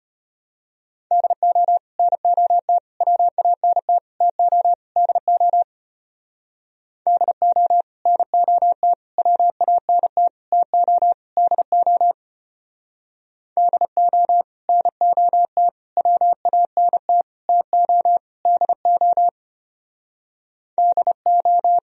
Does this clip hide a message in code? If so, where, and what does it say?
Yes: Morse code from 1.1 to 21.8 s, heard as a keyed tone, about 32 wpm, 700 Hz. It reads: DO NOT WANT TO DO DO NOT WANT TO DO DO NOT WANT TO DO DO